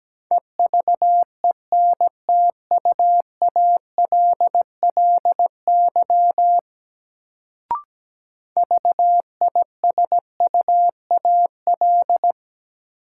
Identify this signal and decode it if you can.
Morse code, heard as a keyed tone, about 17 wpm, 700 Hz. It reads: EVENTUALLY VISUAL
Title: Sentences from Top 1000 Words - Review Part 6 - New Word Drill - 17wpm